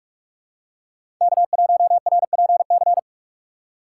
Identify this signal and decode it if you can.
Morse code, heard as a keyed tone, about 46 wpm, 700 Hz. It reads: K1RPC
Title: USA Call Signs - 5 Characters - 4x Character Spacing - 45wpm